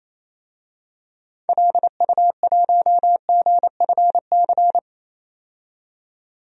Morse code, heard as a keyed tone, about 28 wpm, 700 Hz. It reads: LU1GFC
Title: DXCC-100 Call Signs - 6 Characters - 28wpm